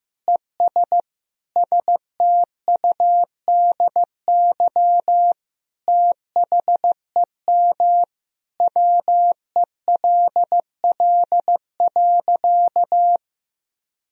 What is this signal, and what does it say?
Morse code, heard as a keyed tone, about 15 wpm, 700 Hz. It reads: ES STUDY THEM WELL.
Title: Sentences from Top 300 Words and Abbreviations 15wpm